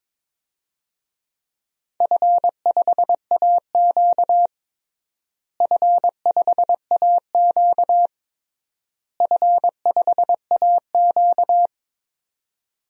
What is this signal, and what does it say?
Morse code, heard as a keyed tone, about 21 wpm, 700 Hz. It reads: F5AQ F5AQ F5AQ
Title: DXCC-100 Call Signs - 4 Characters - Repeated 3x - 22wpm